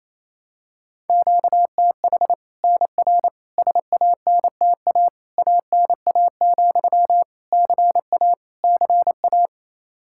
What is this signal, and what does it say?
Morse code, heard as a keyed tone, about 28 wpm, 700 Hz. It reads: QTH NR SANTA ANA, CA CA
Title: Practice QSO - Exchange 1 - Beginning - 28wpm